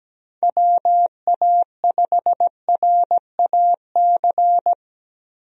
Morse code, heard as a keyed tone, about 17 wpm, 700 Hz. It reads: WA5RAC
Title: USA Call Signs - 6 Characters - 17wpm